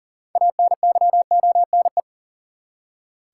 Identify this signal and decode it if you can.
Morse code, heard as a keyed tone, about 40 wpm, 700 Hz. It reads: ANYONE
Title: Six Letter Words - 40wpm